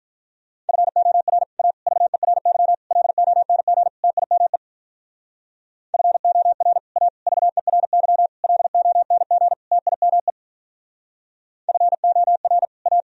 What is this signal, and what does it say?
Morse code, heard as a keyed tone, about 46 wpm, 700 Hz. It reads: FOR A VERY LONG TIME FOR A VERY LONG TIME FOR A